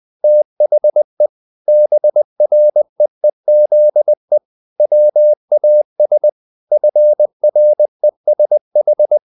Morse code, heard as a keyed tone, about 20 wpm, 600 Hz. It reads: THE BREEZE WAS FRESH
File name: the old man and the sea part 2 20wpm